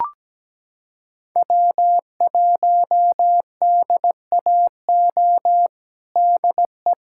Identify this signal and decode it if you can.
Morse code, heard as a keyed tone, about 17 wpm, 700 Hz. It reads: W1DAO DE
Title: USA Callsign DE Callsign - 5 Characters - 17wpm